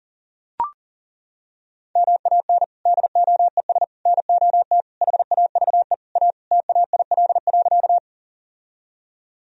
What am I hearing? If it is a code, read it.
Morse code, heard as a keyed tone, about 37 wpm, 700 Hz. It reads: MAN DOES NOT HAVE A TAIL.
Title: Sentences from Top 1000 Words - Review Part 3 - 40wpm